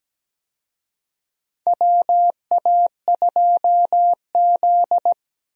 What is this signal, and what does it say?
Morse code, heard as a keyed tone, about 17 wpm, 700 Hz. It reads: WA2Z
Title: USA Call Signs - 4 Characters - 17wpm